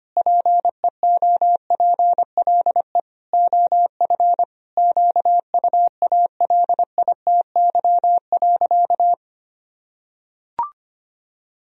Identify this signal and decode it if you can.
Morse code, heard as a keyed tone, about 25 wpm, 700 Hz. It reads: PEOPLE OF QUALITY.
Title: Sentences from Top 1000 Words - Review Part 4 - 25wpm